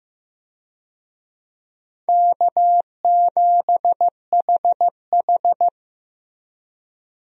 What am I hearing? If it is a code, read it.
Morse code, heard as a keyed tone, about 16 wpm, 700 Hz. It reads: K7HH
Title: USA Call Signs - 4 Characters - Repeated 2x - 15wpm